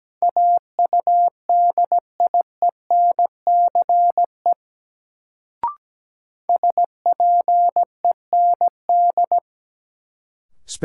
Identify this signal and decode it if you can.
Morse code, heard as a keyed tone, about 17 wpm, 700 Hz. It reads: AUDIENCE SPEND
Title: Sentences from Top 800 Words - New Word Drill - 17wpm